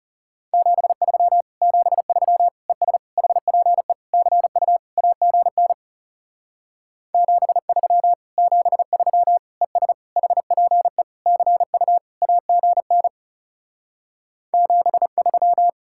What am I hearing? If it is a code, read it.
Morse code, heard as a keyed tone, about 36 wpm, 700 Hz. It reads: 73 73 ES HPE CU AGN 73 73 ES HPE CU AGN 73
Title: Speed-Racing Common QSO Phrases - 30wpm